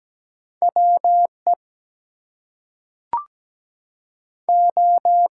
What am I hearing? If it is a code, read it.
Morse code, heard as a keyed tone, about 17 wpm, 700 Hz. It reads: WE O
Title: Two Letter Words - 17wpm